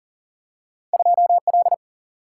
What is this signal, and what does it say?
Morse code, heard as a keyed tone, about 39 wpm, 700 Hz. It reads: 2L
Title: Single-Letter Single-Number 40wpm